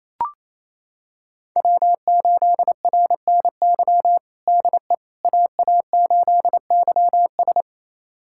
Morse code, heard as a keyed tone, about 28 wpm, 700 Hz. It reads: W8RNY DE AA8YS
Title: USA Callsign DE Callsign 28wpm